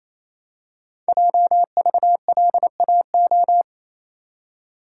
Morse code, heard as a keyed tone, about 27 wpm, 700 Hz. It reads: JVLAO
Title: Code Groups - 5 Letter:Numbers - 28wpm